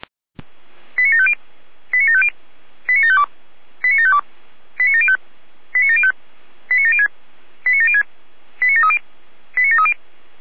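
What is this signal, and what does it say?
Five-tone selective call (ZVEI), heard as quick runs of five tones, tones of about 70 ms. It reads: 89750 89750 89752 89752 89705 89705 89706 89706 89730 89730
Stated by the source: ZVEI five-tone selective call received via SDR.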